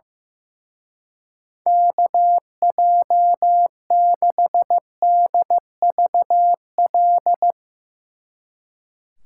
Morse code, heard as a keyed tone, about 15 wpm, 700 Hz. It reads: KJ6DVL
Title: Speed Racing - USA Call Signs 15wpm